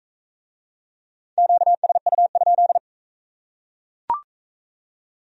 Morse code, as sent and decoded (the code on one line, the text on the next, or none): --.- ... ..- ..--..
QSU?